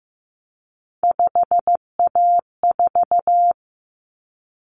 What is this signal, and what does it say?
Morse code, heard as a keyed tone, about 15 wpm, 700 Hz. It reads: 5A4
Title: Common Cut Numbers - 3-Digit 15wpm